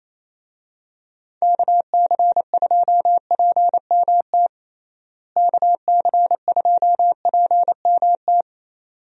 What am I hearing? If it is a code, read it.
Morse code, heard as a keyed tone, about 28 wpm, 700 Hz. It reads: KC2PMT KC2PMT
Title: USA Call Signs - 6 Characters - Repeated 2x - 28wpm